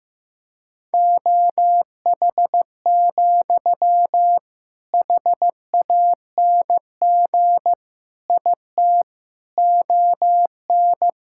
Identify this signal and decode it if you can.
Morse code, heard as a keyed tone, about 16 wpm, 700 Hz. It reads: OH, HANG IT ON
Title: Sentences from Top 1600 Words 15wpm